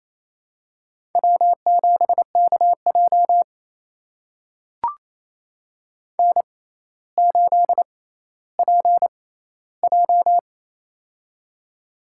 Morse code, heard as a keyed tone, about 28 wpm, spaced as for 14 wpm, 700 Hz. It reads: W7KJ N8PJ
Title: USA Call Signs - 4 Characters - 6x Character Spacing - 28wpm